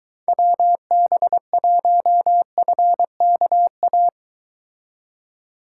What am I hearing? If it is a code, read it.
Morse code, heard as a keyed tone, about 23 wpm, 700 Hz. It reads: WB1FKA